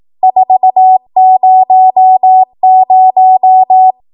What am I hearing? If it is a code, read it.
Morse code, heard as a keyed tone, about 18 wpm, 750 Hz. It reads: 400
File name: Morse / 18WPM